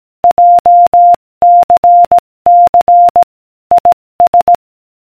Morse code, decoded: JCC IS